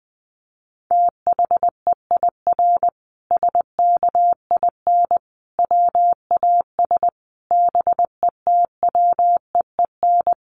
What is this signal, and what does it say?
Morse code, heard as a keyed tone, about 20 wpm, 700 Hz. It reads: THEIR SKIN WAS BETWEEN
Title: Sentences from Top 1300 Words 20wpm